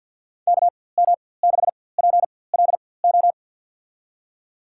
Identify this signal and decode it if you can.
Morse code, heard as a keyed tone, about 48 wpm, spaced as for 23 wpm, 700 Hz. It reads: KM6PLO
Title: USA Call Signs - 6 Characters - 4x Character Spacing - 50wpm